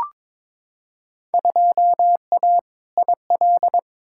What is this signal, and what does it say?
Morse code, heard as a keyed tone, about 22 wpm, 700 Hz. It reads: 2A IL
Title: Field Day Exchanges - 22wpm